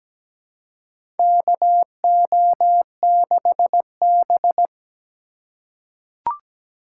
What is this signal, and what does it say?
Morse code, heard as a keyed tone, about 17 wpm, 700 Hz. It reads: KO6B
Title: USA Call Signs - 4x Character Spacing - 17wpm